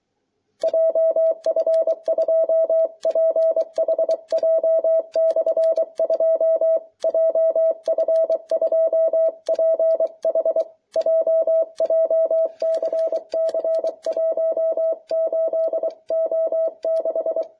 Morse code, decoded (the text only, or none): JF2PHJ/2 JF2PHJ JCC 18O6